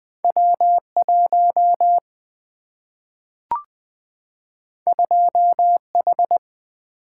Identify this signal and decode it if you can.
Morse code, heard as a keyed tone, about 20 wpm, 700 Hz. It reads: W1 2H
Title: Single-Letter Single-Number 20wpm